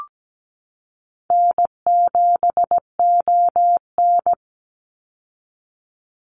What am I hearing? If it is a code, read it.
Morse code, heard as a keyed tone, about 17 wpm, 700 Hz. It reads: N7ON
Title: Call Signs Repeated 3x - 17wpm